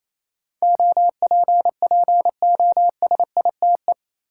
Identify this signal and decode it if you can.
Morse code, heard as a keyed tone, about 28 wpm, 700 Hz. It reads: OPPOSITE
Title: Sentences from Top 1300 Words - New Word Drill Alt - 28wpm